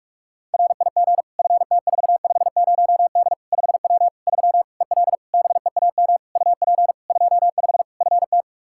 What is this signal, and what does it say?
Morse code, heard as a keyed tone, about 46 wpm, 700 Hz. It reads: RIG FT450D 5W 3 EL BEAM UP 25 FT
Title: Practice QSO - Exchange 2 - Info - 45wpm